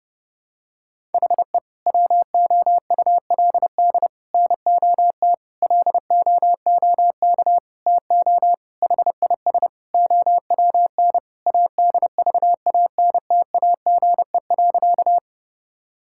Morse code, heard as a keyed tone, about 30 wpm, 700 Hz. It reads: HE WOULD NOT LOOK TO HIS OWN ADVANTAGE.
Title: Sentences from Top 1000 Words - Review Part 6 - 30wpm